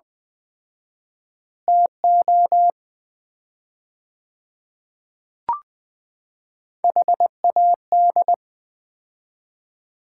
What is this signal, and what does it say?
Morse code, heard as a keyed tone, about 20 wpm, 700 Hz. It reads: TO HAD
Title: 4x Word-Spacing - Sets of 4 Words - 20wpm